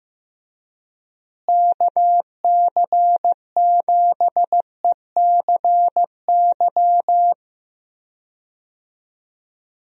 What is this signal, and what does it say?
Morse code, heard as a keyed tone, about 15 wpm, 700 Hz. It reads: KC7ECY
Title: USA Call Signs - 6 Characters - Repeated 3x - 15wpm